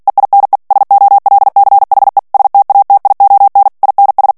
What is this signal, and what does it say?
Morse code, heard as a keyed tone, about 48 wpm, 800 Hz. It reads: EINE SOLCHE STATION INS
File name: morse mp3 / ALLEMAND texte clair 2 vitesse 45 mots minute